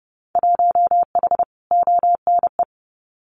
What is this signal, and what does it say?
Morse code, heard as a keyed tone, about 31 wpm, 700 Hz. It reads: 1H ONE
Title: Winter Field Day Exchanges Encore 30wpm